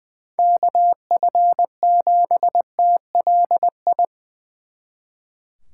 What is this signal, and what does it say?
Morse code, heard as a keyed tone, about 23 wpm, 700 Hz. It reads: KF7TLI
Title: Speed Racing - USA 2x3 Call Signs 20wpm